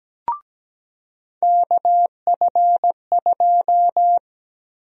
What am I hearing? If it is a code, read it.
Morse code, heard as a keyed tone, about 17 wpm, 700 Hz. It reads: KF2
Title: USA Call Sign Prefixes - 17wpm